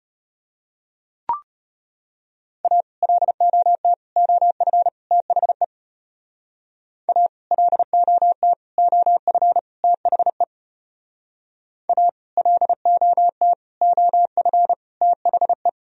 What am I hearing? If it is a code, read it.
Morse code, heard as a keyed tone, about 33 wpm, 700 Hz. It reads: A LOT OF THE A LOT OF THE A LOT OF THE